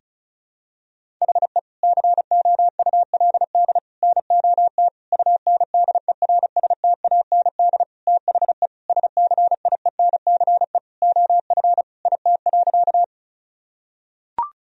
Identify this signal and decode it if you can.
Morse code, heard as a keyed tone, about 34 wpm, 700 Hz. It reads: HE COULD NOT UNDERSTAND THE SCIENCE OF IT.
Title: Sentences from Top 1000 Words - Review Part 2 - 35wpm